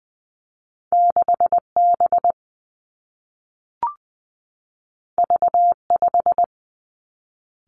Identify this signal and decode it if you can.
Morse code, heard as a keyed tone, about 20 wpm, 700 Hz. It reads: 6 B V 5
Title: Two Challenging Letter-Number Combinations 20wpm